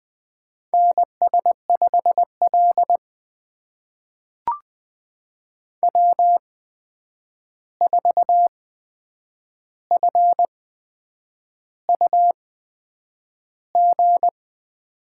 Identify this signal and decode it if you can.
Morse code, heard as a keyed tone, about 20 wpm, spaced as for 8 wpm, 700 Hz. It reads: NS5L W4FUG
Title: USA Call Signs - 8x Character Spacing - 20wpm